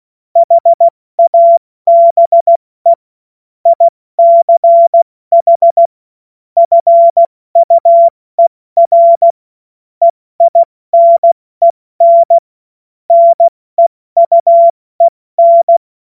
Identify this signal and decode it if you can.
Morse code, heard as a keyed tone, about 16 wpm, 680 Hz. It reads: HABE ICH FUER EINEN NEUEN
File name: Morsekurs / Witze / Witz 06. September 2022 70 F80